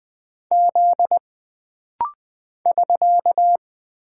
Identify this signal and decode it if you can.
Morse code, heard as a keyed tone, about 20 wpm, 700 Hz. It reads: Z <SK>
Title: Single Character (Letter:Number:Punctuation:Prosign) - Rapid-Fire - 20wpm